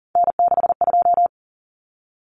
Morse code, heard as a keyed tone, about 40 wpm, 700 Hz. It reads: N62
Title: Common Cut Numbers - 3-Digit 40wpm